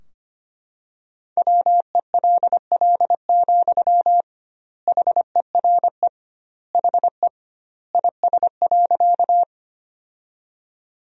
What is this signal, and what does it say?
Morse code, heard as a keyed tone, about 25 wpm, 700 Hz. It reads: WELL, HERE HE IS.
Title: 2x Word-Spacing - Sentences from Top 200 Words - 25wpm